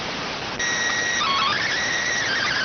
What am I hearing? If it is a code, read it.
SSTV (slow-scan television), PD 290.